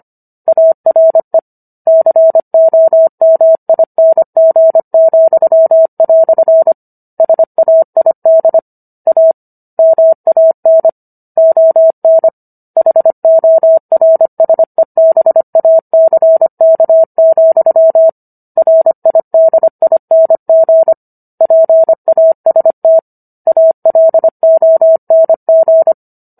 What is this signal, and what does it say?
Morse code, heard as a keyed tone, about 25 wpm, 650 Hz. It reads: ARE COMING," SAID A MAN ON HORSEBACK, RIDING PAST ALONG